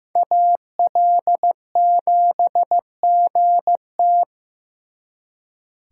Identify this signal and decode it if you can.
Morse code, heard as a keyed tone, about 18 wpm, 700 Hz. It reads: AL7GT
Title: Speed Racing - USA 2x2 Call Signs 15wpm